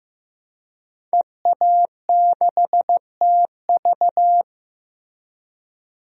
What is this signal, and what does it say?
Morse code, heard as a keyed tone, about 15 wpm, 700 Hz. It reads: EA6TV
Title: DXCC-100 Call Signs - 5 Characters - Repeated 2x - 15wpm